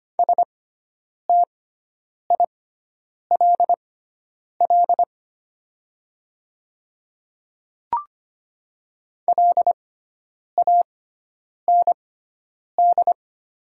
Morse code, heard as a keyed tone, about 25 wpm, spaced as for 9 wpm, 700 Hz. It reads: STILL LAND
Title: 6x Character Spacing - Top 300 Words - 25wpm